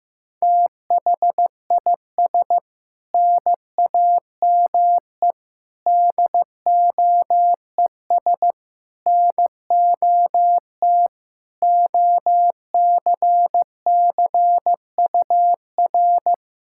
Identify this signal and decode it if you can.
Morse code, heard as a keyed tone, about 15 wpm, 700 Hz. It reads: THIS NAME DOES NOT OCCUR
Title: Sentences from Top 1600 Words 15wpm